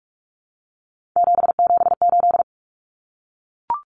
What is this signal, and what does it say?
Morse code, heard as a keyed tone, about 44 wpm, 700 Hz. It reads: 778
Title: Common Cut Numbers - 3-Digit 45wpm